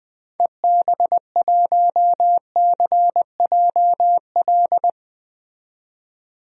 Morse code, heard as a keyed tone, about 20 wpm, 700 Hz. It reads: EB1CJL
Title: DXCC-100 Call Signs - 6 Characters - 20wpm